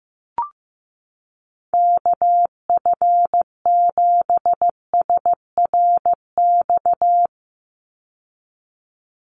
Morse code, heard as a keyed tone, about 15 wpm, 700 Hz. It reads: KF7SRX
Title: USA Call Signs 15wpm